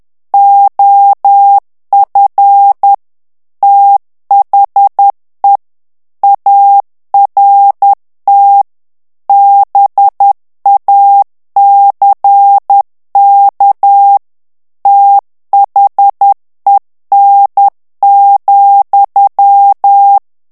Morse code, heard as a keyed tone, about 10 wpm, 800 Hz. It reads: OF THE ART BACK THEN,
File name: ANGLAIS texte clair 3 vitesse 10 mots minute